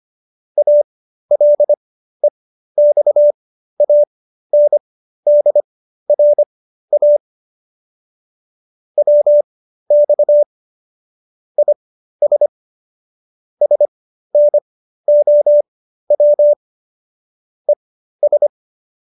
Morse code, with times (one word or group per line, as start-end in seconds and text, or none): 0.6-7.2 ALEXANDRA
9.0-10.4 WX
11.6-12.5 IS
13.6-16.5 SNOW
17.7-18.5 ES